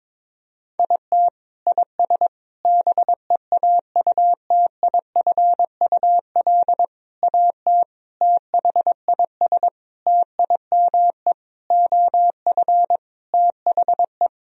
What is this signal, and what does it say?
Morse code, heard as a keyed tone, about 22 wpm, 700 Hz. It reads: IT IS BEAUTIFUL AT THIS TIME OF THE
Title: Sentences from Top 1000 Words - Review Part 1 - 22wpm